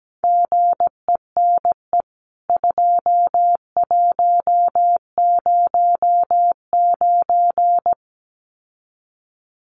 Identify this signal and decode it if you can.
Morse code, heard as a keyed tone, about 17 wpm, 700 Hz. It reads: GENE 2109
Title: CWT Exchange 17wpm